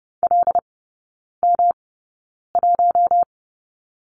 Morse code, heard as a keyed tone, about 30 wpm, spaced as for 9 wpm, 700 Hz. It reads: LM1
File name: Lesson 18e - Letter m - Three Characters 30wpm